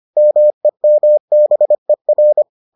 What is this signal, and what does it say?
Morse code, heard as a keyed tone, about 25 wpm, 600 Hz. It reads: MEMBER